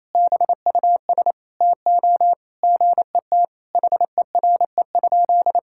Morse code, heard as a keyed tone, about 28 wpm, 700 Hz. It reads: BUS TO GET HERE?
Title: Sentences from Top 1000 Words - Review Part 6 - 28wpm